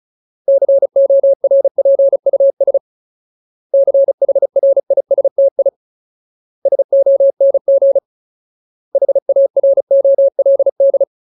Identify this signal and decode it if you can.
Morse code, heard as a keyed tone, about 36 wpm, 550 Hz. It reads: CORPUS CHRISTI SONG HAROLD
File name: Ultimate Word List / 35WPM Ultimate